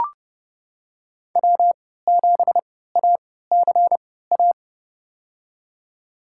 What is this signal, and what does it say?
Morse code, heard as a keyed tone, about 30 wpm, spaced as for 17 wpm, 700 Hz. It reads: W7ACA
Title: USA Call Signs - 5 Characters - 3x Character Spacing - 30wpm